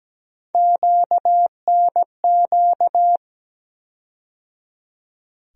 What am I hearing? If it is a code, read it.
Morse code, heard as a keyed tone, about 17 wpm, 700 Hz. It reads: QNQ
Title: USA Call Sign Postfixes - 3 Letters 17wpm